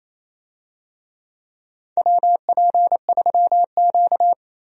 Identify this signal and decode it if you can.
Morse code, heard as a keyed tone, about 28 wpm, 700 Hz. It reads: WP3Q
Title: USA Call Signs - 4 Characters - 28wpm